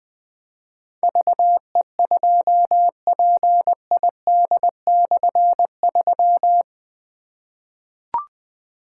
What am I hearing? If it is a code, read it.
Morse code, heard as a keyed tone, about 20 wpm, 700 Hz. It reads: VE2PID/3
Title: K1USN SST QSO Practice - 20wpm